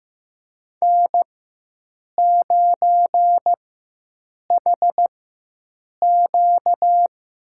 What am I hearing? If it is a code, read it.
Morse code, heard as a keyed tone, about 15 wpm, spaced as for 7 wpm, 700 Hz. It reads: N9HQ
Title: USA Call Signs - 5 Characters - 4x Character Spacing - 15wpm